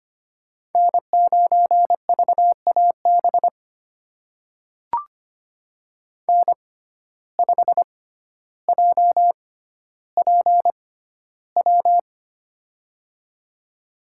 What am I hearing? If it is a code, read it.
Morse code, heard as a keyed tone, about 25 wpm, spaced as for 13 wpm, 700 Hz. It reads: N9VAB N5JPW
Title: USA Call Signs - 5 Characters - 6x Character Spacing - 25wpm